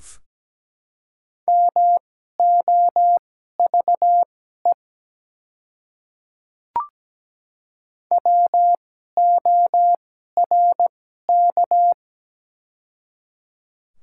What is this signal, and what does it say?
Morse code, heard as a keyed tone, about 17 wpm, spaced as for 12 wpm, 700 Hz. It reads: MOVE WORK
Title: 2x Character Spacing - Top 200 Words - 17wpm